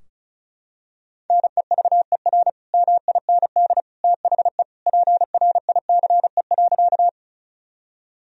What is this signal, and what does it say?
Morse code, heard as a keyed tone, about 36 wpm, 700 Hz. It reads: NEVER MIND THE PRICE.
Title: Sentences from Top 1000 Words - Review Part 4 - 35wpm